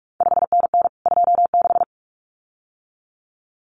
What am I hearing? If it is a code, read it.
Morse code, heard as a keyed tone, about 46 wpm, 700 Hz. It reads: 5NN 26
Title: CQ WW DX Contest Exchange Encore - 45wpm